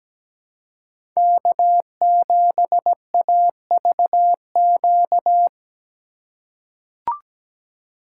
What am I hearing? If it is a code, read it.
Morse code, heard as a keyed tone, about 17 wpm, 700 Hz. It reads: K7AVQ
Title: USA Call Signs - 5 Characters - 6x Character Spacing - 17wpm